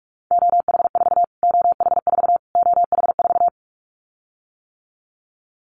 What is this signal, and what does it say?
Morse code, heard as a keyed tone, about 44 wpm, 700 Hz. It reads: OH4 OH4 OH4
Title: Callsign Prefixes for Top 25 DXCC - Repeated 3x - 45wpm